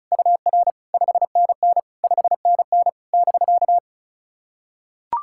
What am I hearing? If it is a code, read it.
Morse code, heard as a keyed tone, about 36 wpm, 700 Hz. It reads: UR 5NN 5NN <BK>
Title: SOTA QSO Practice 35wpm